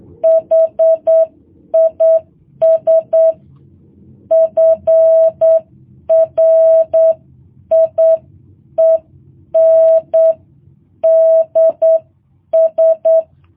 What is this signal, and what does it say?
Morse code, heard as a keyed tone, about 8 wpm, 650 Hz. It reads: HIS FRIENDS
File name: jh8cbh